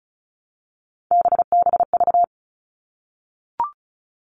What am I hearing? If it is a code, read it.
Morse code, heard as a keyed tone, about 34 wpm, 700 Hz. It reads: BBV
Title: All Cut Numbers - 3-Digit 35wpm